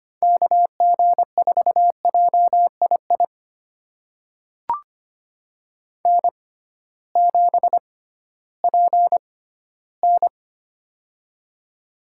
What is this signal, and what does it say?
Morse code, heard as a keyed tone, about 25 wpm, spaced as for 14 wpm, 700 Hz. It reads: KG4JII N7PN
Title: USA Call Signs - 6x Character Spacing - 25wpm